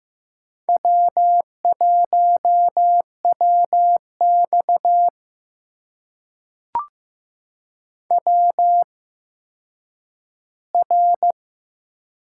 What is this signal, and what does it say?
Morse code, heard as a keyed tone, about 15 wpm, spaced as for 9 wpm, 700 Hz. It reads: W1WX WR
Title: USA Call Signs - 4 Characters - 8x Character Spacing - 15wpm